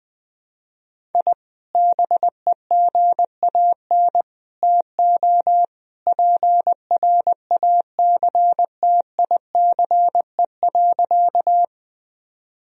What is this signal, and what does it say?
Morse code, heard as a keyed tone, about 20 wpm, 700 Hz. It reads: I BEGAN TO PRACTICE.
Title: Sentences from Top 1000 Words - Review Part 4 - 20wpm